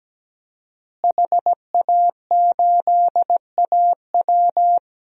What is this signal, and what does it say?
Morse code, heard as a keyed tone, about 17 wpm, 700 Hz. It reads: HA8AW
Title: DXCC-100 Call Signs - 5 Characters - Repeated 3x - 17wpm